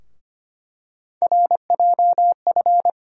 Morse code, heard as a keyed tone, about 25 wpm, 700 Hz. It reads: RJF
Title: USA Call Sign Postfixes - 3 Letters 25wpm